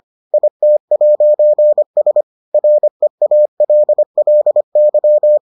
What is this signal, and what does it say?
Morse code, heard as a keyed tone, about 25 wpm, 600 Hz. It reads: IT'S REALLY